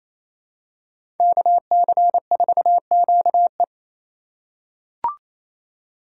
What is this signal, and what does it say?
Morse code, heard as a keyed tone, about 28 wpm, 690 Hz. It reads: KC4QE
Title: USA Call Signs - 3x Character Spacing - 28wpm